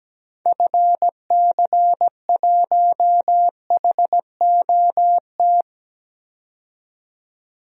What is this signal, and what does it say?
Morse code, heard as a keyed tone, about 17 wpm, 700 Hz. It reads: FC1HOT
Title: DXCC-100 Call Signs - 6 Characters - 17wpm